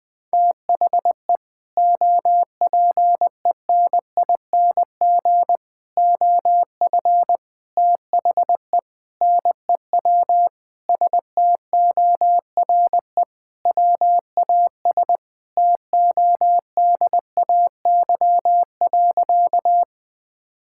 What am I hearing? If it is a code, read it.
Morse code, heard as a keyed tone, about 21 wpm, 700 Hz. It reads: THE OPENING OF THE NEW STORE WAS TODAY.
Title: Sentences with 3-Word Phrases - 20wpm